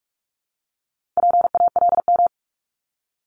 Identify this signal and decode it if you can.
Morse code, heard as a keyed tone, about 46 wpm, 700 Hz. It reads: PALM
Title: Multi-letter Phonemes Repeated 3x 45wpm